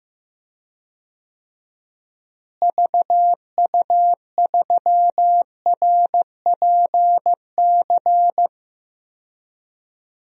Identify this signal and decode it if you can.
Morse code, heard as a keyed tone, about 15 wpm, 700 Hz. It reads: VU3RPC
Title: DXCC-100 Call Signs - 6 Characters - Repeated 2x - 15wpm